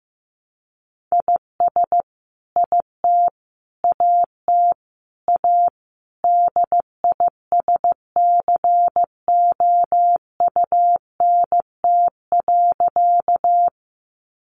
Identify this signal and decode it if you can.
Morse code, heard as a keyed tone, about 15 wpm, 700 Hz. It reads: IS IT AT A DISCOUNT.
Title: Sentences from Top 1200 Words 15wpm